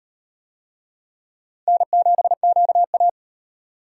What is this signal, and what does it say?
Morse code, heard as a keyed tone, about 38 wpm, 700 Hz. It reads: N7QA